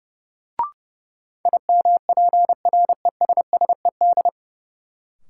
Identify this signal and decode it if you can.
Morse code, heard as a keyed tone, about 30 wpm, 700 Hz. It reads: IMPRESSED
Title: Sentences from Top 1600 Words - New Word Drill Alt - 30wpm